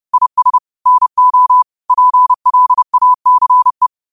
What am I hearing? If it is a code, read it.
Morse code, heard as a keyed tone, about 30 wpm, 1000 Hz. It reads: IS NO PLACE
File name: qotdfiles / 30wpm